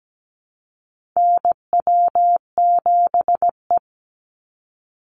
Morse code, heard as a keyed tone, about 17 wpm, 700 Hz. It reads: NW7E
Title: NA SOTA Chaser Callsigns 17wpm